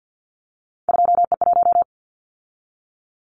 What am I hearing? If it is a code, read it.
Morse code, heard as a keyed tone, about 50 wpm, 700 Hz. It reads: 2E1
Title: Callsign Prefixes for Top 25 DXCC - 50wpm